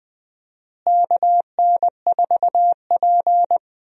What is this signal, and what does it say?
Morse code, heard as a keyed tone, about 20 wpm, 700 Hz. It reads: KN4P
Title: USA Call Signs - 4 Characters - Repeated 3x - 20wpm